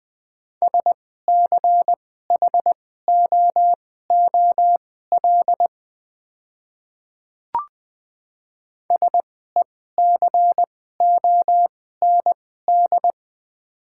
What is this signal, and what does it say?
Morse code, heard as a keyed tone, about 20 wpm, spaced as for 15 wpm, 700 Hz. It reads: SCHOOL SECOND
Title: 2x Character Spacing - Top 300 Words - 20wpm